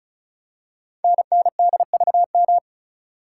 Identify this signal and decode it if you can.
Morse code, heard as a keyed tone, about 36 wpm, 700 Hz. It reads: NNDVM
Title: Code Groups - 5 Letters - 35wpm